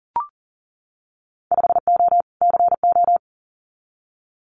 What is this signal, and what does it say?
Morse code, heard as a keyed tone, about 41 wpm, 700 Hz. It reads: 5O CO
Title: Winter Field Day Exchanges Encore 40wpm